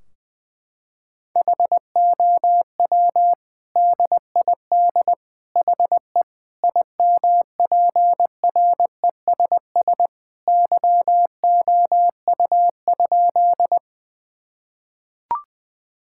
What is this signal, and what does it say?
Morse code, heard as a keyed tone, about 20 wpm, 700 Hz. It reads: HOW DID HE IMPRESS YOU?
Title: Sentences from Top 1600 Words 20wpm